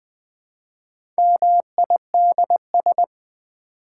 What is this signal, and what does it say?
Morse code, heard as a keyed tone, about 20 wpm, 700 Hz. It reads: MIDS
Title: Build A Word - 1 Letter At A Time - 5-Letter Words - 20wpm